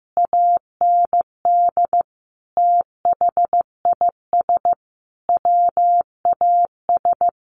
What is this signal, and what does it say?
Morse code, heard as a keyed tone, about 15 wpm, 700 Hz. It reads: AND THIS WAS
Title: A Sound Of Thunder 15wpm